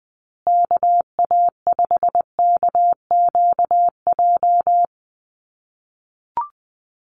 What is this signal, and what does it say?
Morse code, heard as a keyed tone, about 20 wpm, 700 Hz. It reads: KA5KQJ
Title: USA Call Signs Repeated 2x 20wpm